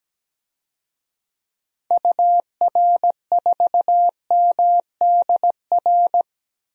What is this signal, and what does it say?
Morse code, heard as a keyed tone, about 17 wpm, 700 Hz. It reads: UR4MDR
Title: DXCC-100 Call Signs - 6 Characters - Repeated 3x - 17wpm